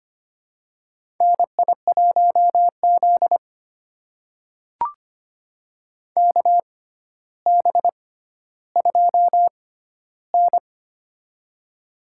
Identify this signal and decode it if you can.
Morse code, heard as a keyed tone, about 25 wpm, spaced as for 13 wpm, 700 Hz. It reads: NI1Z KB2N
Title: USA Call Signs - 4 Characters - 6x Character Spacing - 25wpm